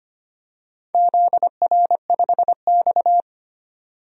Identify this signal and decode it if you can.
Morse code, heard as a keyed tone, about 25 wpm, 700 Hz. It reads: ZR5X
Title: DXCC-100 Call Signs - 4 Characters - 25wpm